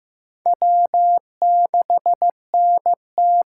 Morse code, heard as a keyed tone, about 18 wpm, 700 Hz. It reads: W6NT
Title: Speed Racing - USA 1x2 Call Signs 15wpm